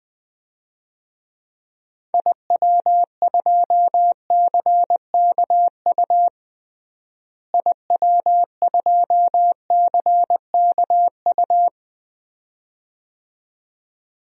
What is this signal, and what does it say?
Morse code, heard as a keyed tone, about 20 wpm, 700 Hz. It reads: IW2CKU IW2CKU
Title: DXCC-100 Call Signs - 6 Characters - Repeated 2x - 20wpm